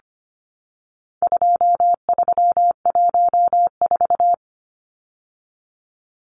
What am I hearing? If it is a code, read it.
Morse code, heard as a keyed tone, about 25 wpm, 700 Hz. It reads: 2314
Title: Numbers - 4-Digit 25wpm